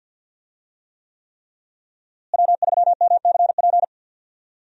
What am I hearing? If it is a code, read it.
Morse code, heard as a keyed tone, about 46 wpm, 700 Hz. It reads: W3MCP
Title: USA Call Signs - 5 Characters - Repeated 3x - 50wpm